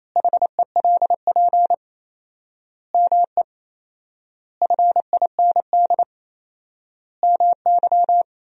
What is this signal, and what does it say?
Morse code, heard as a keyed tone, about 28 wpm, 700 Hz. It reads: HELP ME FIND MY
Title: 4x Word-Spacing - Sentences from Top 200 Words - 28wpm